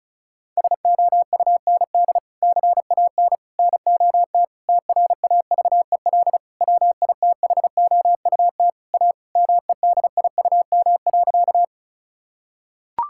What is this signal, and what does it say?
Morse code, heard as a keyed tone, about 36 wpm, 700 Hz. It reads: SOUND CAN NOT TRAVEL WITHOUT A MEDIUM.
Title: Sentences from Top 1000 Words - Review Part 6 - 35wpm